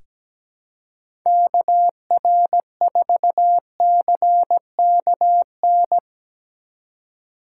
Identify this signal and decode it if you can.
Morse code, heard as a keyed tone, about 17 wpm, 700 Hz. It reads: KR4CKN
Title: USA Call Signs - 2x Character Spacing - 17wpm